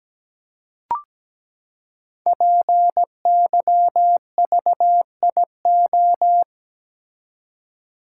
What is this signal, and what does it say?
Morse code, heard as a keyed tone, about 17 wpm, 700 Hz. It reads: PYVIO
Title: Code Groups - 5 Letter:Numbers - 17wpm